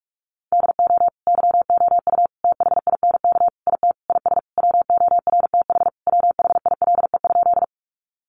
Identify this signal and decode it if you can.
Morse code, heard as a keyed tone, about 44 wpm, 700 Hz. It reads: DO YOU THINK IT IS WORTH WHILE?
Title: Sentences from Top 800 Words 45wpm